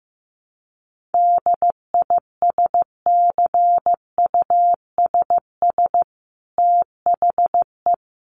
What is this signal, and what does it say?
Morse code, heard as a keyed tone, about 14 wpm, 700 Hz. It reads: DISCUSS THE
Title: Sentences from Top 1500 Words 15wpm